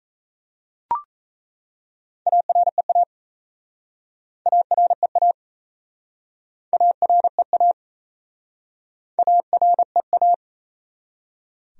Morse code, heard as a keyed tone, about 38 wpm, 700 Hz. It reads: AREA AREA AREA AREA